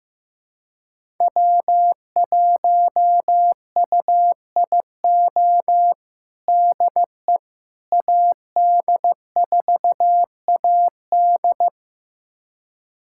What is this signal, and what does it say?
Morse code, heard as a keyed tone, about 16 wpm, 700 Hz. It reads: W1UIO DE AD4AD
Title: USA Callsign DE Callsign - 5 Characters - 15wpm